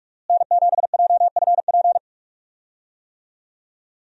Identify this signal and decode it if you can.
Morse code, heard as a keyed tone, about 46 wpm, 700 Hz. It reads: N7JFP
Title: USA Call Signs - 5 Characters - Repeated 3x - 45wpm